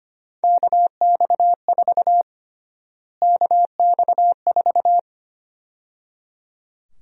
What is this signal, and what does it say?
Morse code, heard as a keyed tone, about 25 wpm, 700 Hz. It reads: KX4 KX4
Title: USA Call Sign Prefixes - Repeated 2x - 25wpm